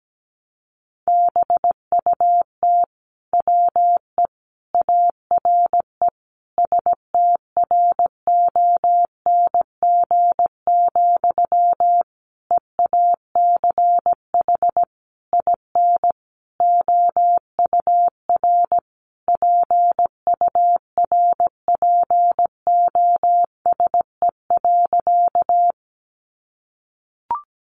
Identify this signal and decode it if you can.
Morse code, heard as a keyed tone, about 17 wpm, 700 Hz. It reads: BUT WE ARE STRONG, EACH IN OUR PURPOSE.
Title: Sentences from Top 800 Words 17wpm